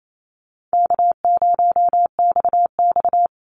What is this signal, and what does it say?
Morse code, heard as a keyed tone, about 28 wpm, 700 Hz. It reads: K0XX
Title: Easy Call Signs - 28wpm